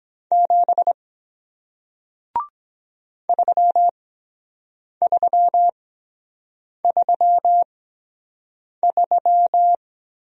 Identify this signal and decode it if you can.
Morse code, heard as a keyed tone, about 23 wpm, spaced as for 7 wpm, 700 Hz. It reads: 7 3333